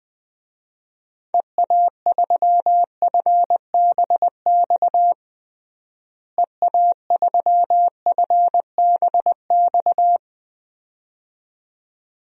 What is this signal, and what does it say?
Morse code, heard as a keyed tone, about 20 wpm, 700 Hz. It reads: EA3FBX EA3FBX
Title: DXCC-100 Call Signs - 6 Characters - Repeated 2x - 20wpm